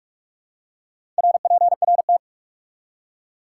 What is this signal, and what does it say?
Morse code, heard as a keyed tone, about 46 wpm, 700 Hz. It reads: RPRT
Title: Top 90 QSO Elements 45wpm